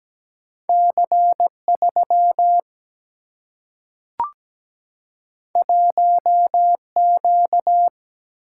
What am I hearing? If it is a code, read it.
Morse code, heard as a keyed tone, about 17 wpm, 700 Hz. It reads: C 3 1 Q
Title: Single-Letter Single-Number 17wpm